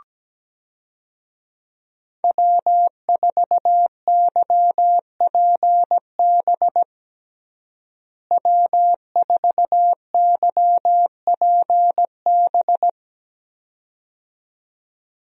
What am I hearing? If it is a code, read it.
Morse code, heard as a keyed tone, about 17 wpm, 700 Hz. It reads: W4YPB W4YPB
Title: USA Call Signs - 5 Characters - Repeated 2x - 17wpm